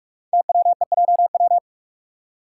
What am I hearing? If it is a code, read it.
Morse code, heard as a keyed tone, about 44 wpm, 690 Hz. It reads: TWEJW
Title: Code Groups - 5 Letters - 45wpm